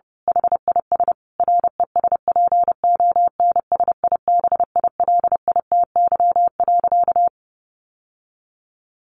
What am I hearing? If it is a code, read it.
Morse code, heard as a keyed tone, about 28 wpm, 700 Hz. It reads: HIS RESPONSIBILITY.
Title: Sentences from Top 1200 Words 30wpm